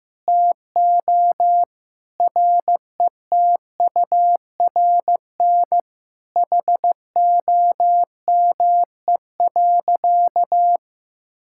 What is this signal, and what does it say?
Morse code, heard as a keyed tone, about 15 wpm, 700 Hz. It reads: TO RETURN HOME.
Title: Sentences from Top 1600 Words 15wpm